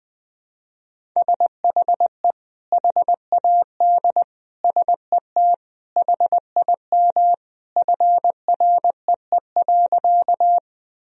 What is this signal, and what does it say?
Morse code, heard as a keyed tone, about 20 wpm, 700 Hz. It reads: SHE HAD SET HIM FREE.
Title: Sentences from Top 1000 Words - Review Part 1 - 20wpm